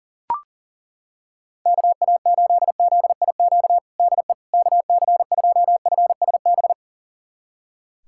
Morse code, heard as a keyed tone, about 37 wpm, 700 Hz. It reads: KA8ZIQ DE KC2FSB
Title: USA Callsign DE Callsign 40wpm